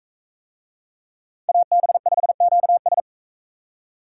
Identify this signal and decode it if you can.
Morse code, heard as a keyed tone, about 41 wpm, 700 Hz. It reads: AB5QS